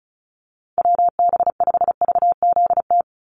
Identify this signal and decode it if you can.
Morse code, heard as a keyed tone, about 34 wpm, 700 Hz. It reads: WB5VZT
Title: USA Call Signs 35wpm